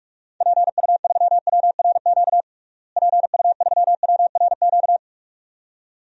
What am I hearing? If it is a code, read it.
Morse code, heard as a keyed tone, about 42 wpm, 700 Hz. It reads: PU3WRQ PU3WRQ
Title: DXCC-100 Call Signs - 6 Characters - Repeated 2x - 45wpm